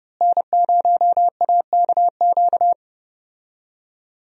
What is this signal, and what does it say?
Morse code, heard as a keyed tone, about 31 wpm, 700 Hz. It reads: N0AKQ
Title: USA Call Signs - 5 Characters - Repeated 3x - 30wpm